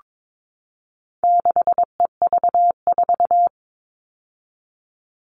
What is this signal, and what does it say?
Morse code, heard as a keyed tone, about 22 wpm, 700 Hz. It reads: 6EV4
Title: All Cut Numbers - 4-Digit 22wpm